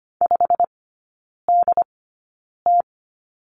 Morse code, heard as a keyed tone, about 25 wpm, 700 Hz. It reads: 5 D T
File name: course / 25:10wpm / 100 Lesson 17e - Letter u - Three Characters 25wpm